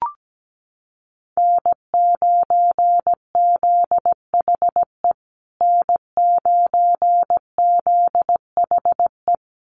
Morse code, heard as a keyed tone, about 17 wpm, 700 Hz. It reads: N9ZHE N9ZHE
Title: USA Call Signs Repeated 2x 17wpm